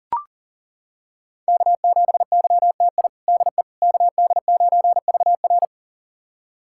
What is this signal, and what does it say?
Morse code, heard as a keyed tone, about 37 wpm, 700 Hz. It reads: K7YTI DE KD9VR
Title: USA Callsign DE Callsign - 5 Characters - 40wpm